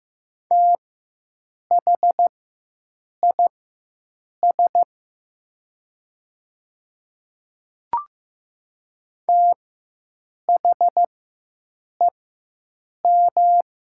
Morse code, heard as a keyed tone, about 15 wpm, spaced as for 7 wpm, 700 Hz. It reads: THIS THEM
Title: 4x Character Spacing - Top 100 Words - 15wpm